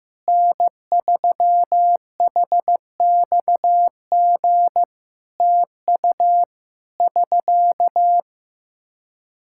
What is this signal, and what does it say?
Morse code, heard as a keyed tone, about 14 wpm, 700 Hz. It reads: N3HXG TU <SK>
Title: Practice QSO - Exchange 4 - Ending - 15wpm